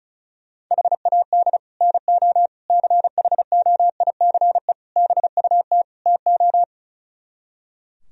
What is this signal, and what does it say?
Morse code, heard as a keyed tone, about 44 wpm, 700 Hz. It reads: HAD NO CHOICE BUT TO